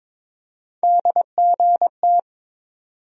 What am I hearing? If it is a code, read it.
Morse code, heard as a keyed tone, about 22 wpm, 700 Hz. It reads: DGT
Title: USA Call Sign Postfixes - 3 Letters 22wpm